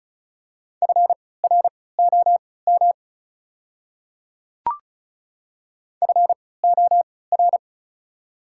Morse code, heard as a keyed tone, about 34 wpm, spaced as for 20 wpm, 690 Hz. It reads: FROM FOR
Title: 3x Character Spacing - Top 100 Words - 35wpm